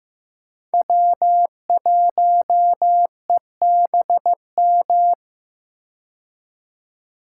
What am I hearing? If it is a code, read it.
Morse code, heard as a keyed tone, about 15 wpm, 700 Hz. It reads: W1EBM
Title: K1USN SST QSO Practice - 15wpm